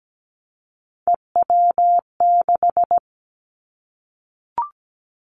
Morse code, heard as a keyed tone, about 17 wpm, 700 Hz. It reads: EW6
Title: Callsign Prefixes for Top 50 DXCC - 17wpm